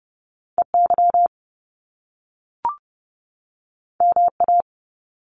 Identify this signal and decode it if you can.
Morse code, heard as a keyed tone, about 30 wpm, 700 Hz. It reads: EY MA
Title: Top Two Letter Combinations in Words 30wpm